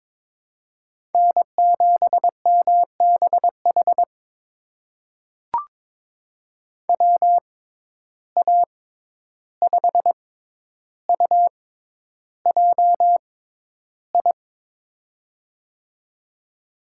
Morse code, heard as a keyed tone, about 22 wpm, spaced as for 11 wpm, 700 Hz. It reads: N7MBH WA5UJI
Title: USA Call Signs - 6x Character Spacing - 22wpm